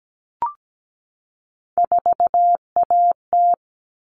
Morse code, heard as a keyed tone, about 17 wpm, 700 Hz. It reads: 4AT
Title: All Cut Numbers - 3-Digit 17wpm